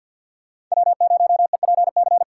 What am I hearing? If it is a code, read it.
Morse code, heard as a keyed tone, about 51 wpm, 700 Hz. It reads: W0EPC
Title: K1USN SST QSO Practice - 50wpm